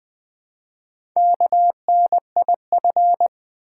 Morse code, heard as a keyed tone, about 20 wpm, 700 Hz. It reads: KNIF
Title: Build A Word - 1 Letter At A Time - 5-Letter Words - 20wpm